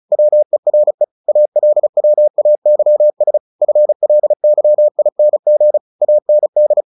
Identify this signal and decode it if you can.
Morse code, heard as a keyed tone, about 34 wpm, 600 Hz. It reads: WERE ALWAYS FLYING AND
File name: the old man and the sea / the old man and the sea 1 35wpm